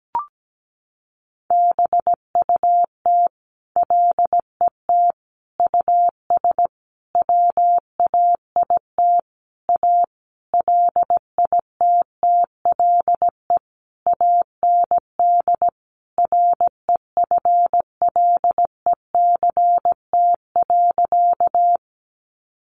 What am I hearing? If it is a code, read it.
Morse code, heard as a keyed tone, about 17 wpm, 700 Hz. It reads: BUT LET US WAIT A LITTLE AND REFLECT.
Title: Sentences from Top 1500 Words 17wpm